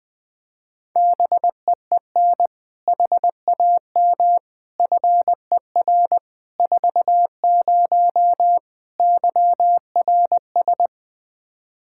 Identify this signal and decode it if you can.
Morse code, heard as a keyed tone, about 20 wpm, 700 Hz. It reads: BEEN HAM FER 40 YRS
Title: Practice QSO - Exchange 3 - More Info - 20wpm